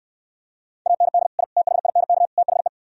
Morse code, heard as a keyed tone, about 69 wpm, 700 Hz. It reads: AND I THINK THE